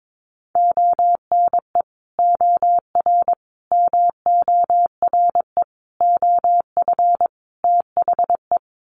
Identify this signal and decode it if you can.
Morse code, heard as a keyed tone, about 22 wpm, 700 Hz. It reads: ONE OR MORE OF THE
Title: Sets of 5 Words Encore 22wpm